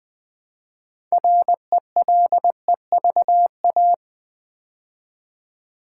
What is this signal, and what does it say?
Morse code, heard as a keyed tone, about 20 wpm, 700 Hz. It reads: RELEVA
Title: Build A Word - 3 Letters At A Time - 9-Letter Words - 20wpm